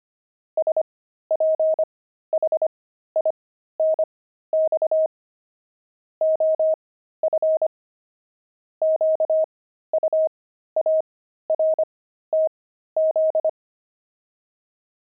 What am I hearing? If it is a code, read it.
Morse code, heard as a keyed tone, about 25 wpm, spaced as for 13 wpm, 630 Hz. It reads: SPHINX OF QUARTZ
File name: pangram-fs-13-cs-25-jackdaws